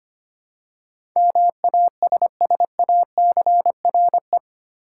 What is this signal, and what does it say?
Morse code, heard as a keyed tone, about 25 wpm, 700 Hz. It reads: MASSACRE
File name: 8 Letter Words - First 3 Letters Given - 25wpm